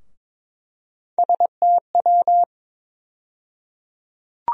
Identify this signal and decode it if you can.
Morse code, heard as a keyed tone, about 22 wpm, 700 Hz. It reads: STW
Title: USA Call Sign Postfixes - 3 Letters 22wpm